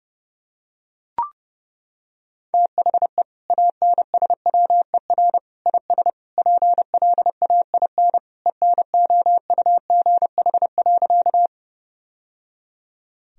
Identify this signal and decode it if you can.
Morse code, heard as a keyed tone, about 30 wpm, 700 Hz. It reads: THE ANSWER IS PLAIN ENOUGH.
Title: Sentences from Top 1000 Words - Review Part 2 - 30wpm